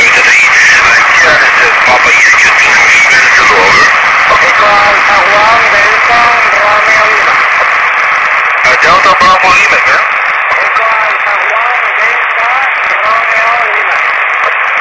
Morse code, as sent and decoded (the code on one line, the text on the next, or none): none